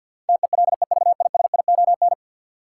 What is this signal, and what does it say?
Morse code, heard as a keyed tone, about 52 wpm, 700 Hz. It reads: TELEVISION
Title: Sentences from Top 1000 Words - Review Part 4 - New Word Drill - 50wpm